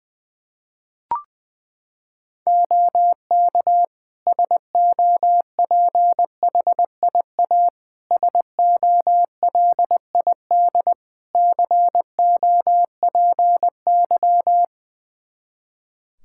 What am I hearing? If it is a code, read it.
Morse code, heard as a keyed tone, about 20 wpm, 700 Hz. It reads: OK SOPHIA SOLID COPY
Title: Practice QSO - Exchange 3 - More Info - 20wpm